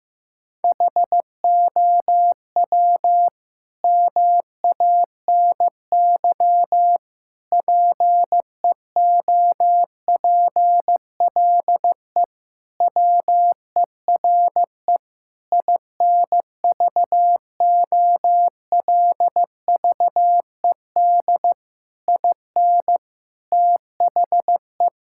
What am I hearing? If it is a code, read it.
Morse code, heard as a keyed tone, about 15 wpm, 700 Hz. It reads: HOW MANY PEOPLE WERE INVOLVED IN THE
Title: Sentences from Top 1000 Words - Review Part 5 - 15wpm